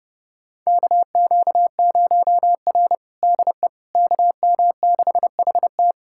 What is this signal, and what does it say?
Morse code, heard as a keyed tone, about 32 wpm, 700 Hz. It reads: KQ0R DE KM6HT
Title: USA Callsign DE Callsign 30wpm